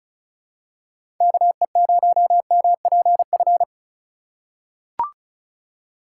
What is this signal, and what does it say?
Morse code, heard as a keyed tone, about 34 wpm, 700 Hz. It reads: KE0MPF
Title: USA Call Signs - 6 Characters - 4x Character Spacing - 35wpm